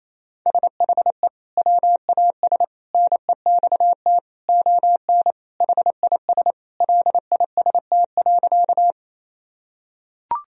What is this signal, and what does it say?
Morse code, heard as a keyed tone, about 28 wpm, 700 Hz. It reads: SHE WAS NEXT ON HIS LIST.
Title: Sentences from Top 1000 Words - Review Part 2 - 28wpm